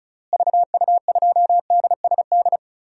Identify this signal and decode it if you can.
Morse code, heard as a keyed tone, about 36 wpm, 700 Hz. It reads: VU2DSD
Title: DXCC-100 Call Signs - 6 Characters - Repeated 2x - 35wpm